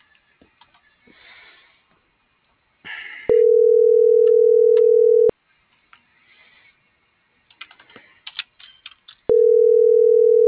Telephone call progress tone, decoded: ringback tone

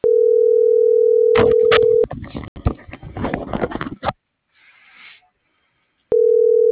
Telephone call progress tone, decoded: ringback tone

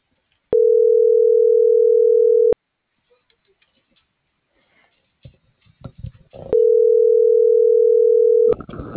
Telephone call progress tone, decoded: ringback tone

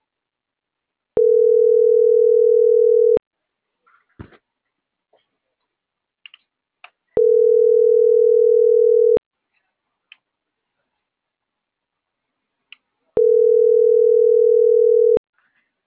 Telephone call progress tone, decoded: ringback tone

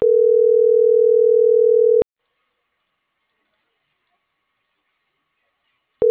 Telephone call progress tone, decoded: ringback tone

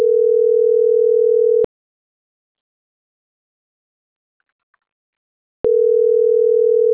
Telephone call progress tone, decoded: ringback tone

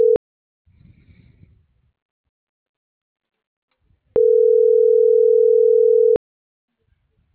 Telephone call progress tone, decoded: ringback tone

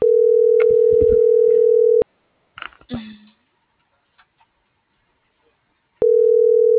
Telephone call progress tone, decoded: ringback tone

